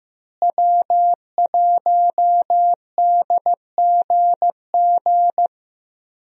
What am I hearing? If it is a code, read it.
Morse code, heard as a keyed tone, about 15 wpm, 700 Hz. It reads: W1DGG
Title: USA Call Signs - 5 Characters - Repeated 3x - 15wpm